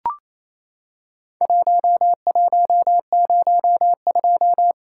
Morse code, heard as a keyed tone, about 28 wpm, 700 Hz. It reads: 1102
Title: Worked All Europe DX Contest - QTC Exchange - 28wpm